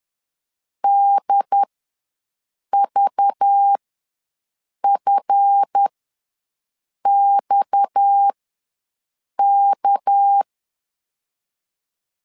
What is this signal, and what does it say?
Morse code, heard as a keyed tone, about 10 wpm, spaced as for 6 wpm, 800 Hz. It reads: DVFXK
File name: LEZIONI / 7 GF9Z / 7 GF ws 10wpm 3918